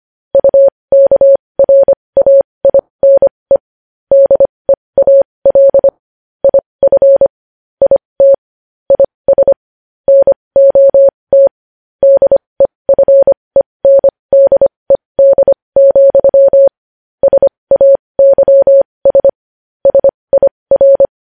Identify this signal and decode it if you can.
Morse code, heard as a keyed tone, about 25 wpm, spaced as for 20 wpm, 550 Hz. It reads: UKRAINE DEAL IF IT IS NOT DEFENDED, SAYS SIR
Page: news-headlines-in-morse-code-at-20-wpm